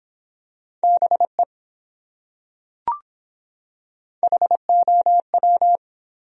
Morse code, heard as a keyed tone, about 23 wpm, 700 Hz. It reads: BE HOW